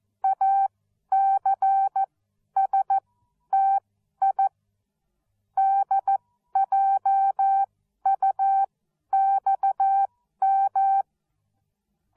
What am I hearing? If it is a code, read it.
Morse code, heard as a keyed tone, about 14 wpm, spaced as for 11 wpm, 780 Hz. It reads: ACSTI DJUXM
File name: Onlinelehrgang / morsen / DJ4UF Morsen13